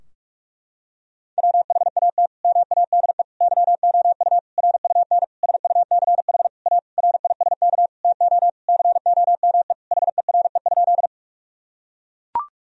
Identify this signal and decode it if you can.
Morse code, heard as a keyed tone, about 41 wpm, 700 Hz. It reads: WHAT MADE YOU RUN SUCH A RISK TO COME HERE?
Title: Sentences from Top 1000 Words - Review Part 4 - 45wpm